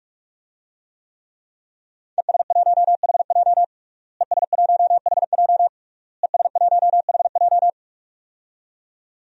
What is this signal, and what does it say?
Morse code, heard as a keyed tone, about 45 wpm, 700 Hz. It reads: ES1HJ ES1HJ ES1HJ
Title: DXCC-100 Call Signs - 5 Characters - Repeated 3x - 45wpm